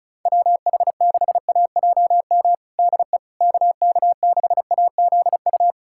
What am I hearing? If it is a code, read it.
Morse code, heard as a keyed tone, about 34 wpm, 700 Hz. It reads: WH6AJM DE KK6AZU
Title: USA Callsign DE Callsign 35wpm